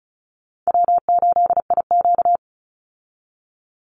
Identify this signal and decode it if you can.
Morse code, heard as a keyed tone, about 34 wpm, 700 Hz. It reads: W8IQ
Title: Easy Call Signs Repeated 2x - 35wpm